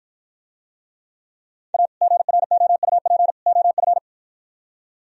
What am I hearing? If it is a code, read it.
Morse code, heard as a keyed tone, about 54 wpm, 700 Hz. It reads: A GROUP OF